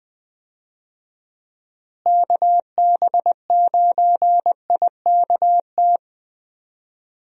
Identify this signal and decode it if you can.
Morse code, heard as a keyed tone, about 20 wpm, 700 Hz. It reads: KB9IKT
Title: USA Call Signs - 6 Characters - Repeated 3x - 20wpm